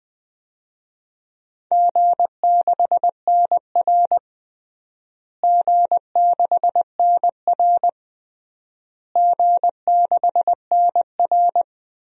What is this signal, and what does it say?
Morse code, heard as a keyed tone, about 20 wpm, 700 Hz. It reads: G6NR G6NR G6NR
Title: DXCC-100 Call Signs - 4 Characters - Repeated 3x - 20wpm